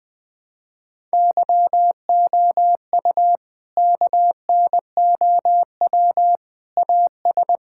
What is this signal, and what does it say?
Morse code, heard as a keyed tone, about 19 wpm, 700 Hz. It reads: YOU KNOW AS
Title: Sentences with 8-Word Phrases - 20wpm